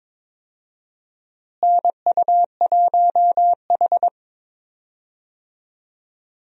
Morse code, heard as a keyed tone, about 22 wpm, 700 Hz. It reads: NU1H
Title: USA Call Signs - 4 Characters - Repeated 3x - 22wpm